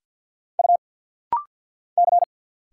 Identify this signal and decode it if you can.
Morse code, heard as a keyed tone, about 50 wpm, 700 Hz. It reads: U C
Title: Single Letter-Number - Rapid-Fire 50wpm